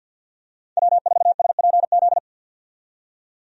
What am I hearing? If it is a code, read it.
Morse code, heard as a keyed tone, about 48 wpm, 700 Hz. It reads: W4SPZ
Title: USA Call Signs - 5 Characters - 50wpm